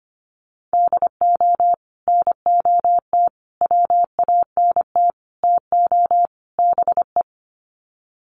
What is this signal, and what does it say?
Morse code, heard as a keyed tone, about 25 wpm, 700 Hz. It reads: DO NOT WANT TO BE
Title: Sets of 5 Words 25wpm